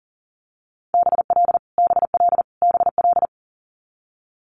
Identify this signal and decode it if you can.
Morse code, heard as a keyed tone, about 41 wpm, 700 Hz. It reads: BL BL BL
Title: Multi-letter Phonemes Repeated 3x 40wpm